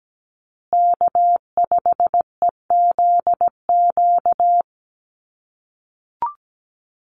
Morse code, heard as a keyed tone, about 17 wpm, 700 Hz. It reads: K5EZQ
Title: USA Call Signs Repeated 2x 17wpm